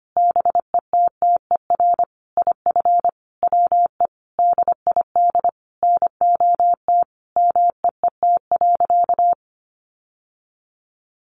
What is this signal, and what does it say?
Morse code, heard as a keyed tone, about 25 wpm, 700 Hz. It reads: BETTER IF WE DID NOT MEET.
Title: Sentences from Top 1100 Words 25wpm